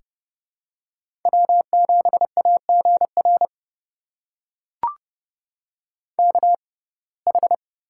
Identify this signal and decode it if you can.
Morse code, heard as a keyed tone, about 30 wpm, spaced as for 22 wpm, 700 Hz. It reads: W7AGR KH
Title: USA Call Signs - 5 Characters - 6x Character Spacing - 30wpm